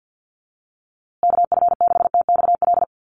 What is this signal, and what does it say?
Morse code, heard as a keyed tone, about 50 wpm, 700 Hz. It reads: KF6TXL
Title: USA Call Signs Repeated 3x 50wpm